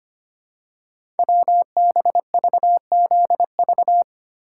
Morse code, heard as a keyed tone, about 25 wpm, 700 Hz. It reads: WBVZV
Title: Code Groups - 5 Letters - 25wpm